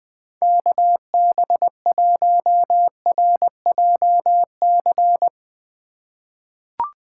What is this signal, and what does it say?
Morse code, heard as a keyed tone, about 21 wpm, 700 Hz. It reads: KB1RJC
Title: 2023 SOTA Chaser Call Signs - 20wpm